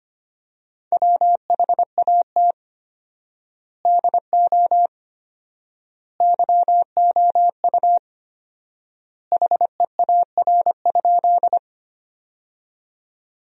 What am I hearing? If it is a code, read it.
Morse code, heard as a keyed tone, about 25 wpm, 700 Hz. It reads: WHAT DO YOU HEAR?
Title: 4x Word-Spacing - Sentences from Top 300 Words - 25wpm